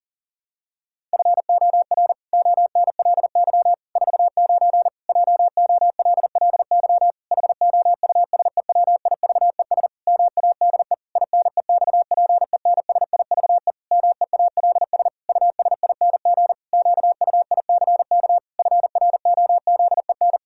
Morse code, heard as a keyed tone, about 40 wpm, 700 Hz. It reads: FOR ONLY 49 JOLLY HOUSEWIVES MADE INEXPENSIVE MEALS USING QUICK FROZEN
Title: Pangram Challenge v2 - 40wpm